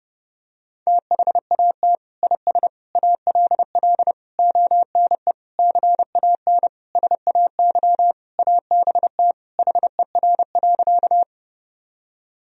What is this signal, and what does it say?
Morse code, heard as a keyed tone, about 30 wpm, 700 Hz. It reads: THAT IS ALL ONE CAN SAY ABT HER.
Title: Sentences from Top 200 Words and Abbreviations 30wpm